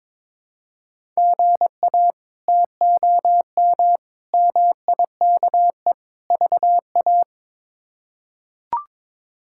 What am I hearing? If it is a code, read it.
Morse code, heard as a keyed tone, about 22 wpm, 700 Hz. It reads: GA TOM MIKE VA
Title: K1USN SST QSO Practice - 22wpm